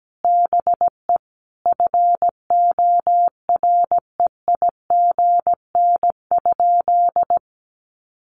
Morse code, heard as a keyed tone, about 17 wpm, 700 Hz. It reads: BE FOREIGN?
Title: Sentences from Top 1400 Words 17wpm